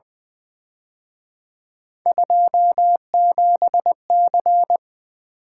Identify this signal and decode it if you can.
Morse code, heard as a keyed tone, about 24 wpm, 700 Hz. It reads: 27C
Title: Speed-Racing - Three Letter-Number Combinations - 20wpm